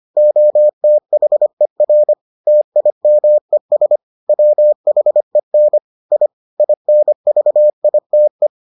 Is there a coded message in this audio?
Yes: Morse code from 0.2 to 8.5 s, heard as a keyed tone, about 25 wpm, 600 Hz. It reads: OTHER TIMES WHEN I INVITE